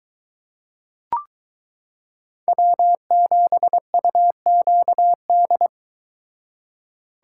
Morse code, heard as a keyed tone, about 23 wpm, 700 Hz. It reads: W7UQD